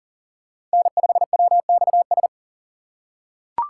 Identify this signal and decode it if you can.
Morse code, heard as a keyed tone, about 40 wpm, 700 Hz. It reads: N5WXS
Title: USA Call Signs - 5 Characters - 3x Character Spacing - 40wpm